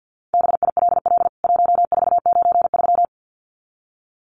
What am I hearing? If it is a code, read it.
Morse code, heard as a keyed tone, about 50 wpm, 700 Hz. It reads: BILL 1493
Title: CWT Exchange 50wpm